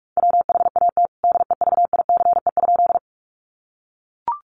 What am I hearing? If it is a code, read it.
Morse code, heard as a keyed tone, about 44 wpm, 700 Hz. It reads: WHAT DEVICE?
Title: Sentences from Top 1500 Words 45wpm